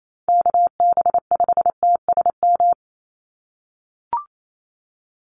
Morse code, heard as a keyed tone, about 28 wpm, 700 Hz. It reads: KB5TSM
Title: USA Call Signs 28wpm